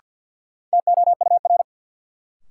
Morse code, heard as a keyed tone, about 52 wpm, 700 Hz. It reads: TOUR
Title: Sentences from Top 1400 Words - New Word Drill - 50wpm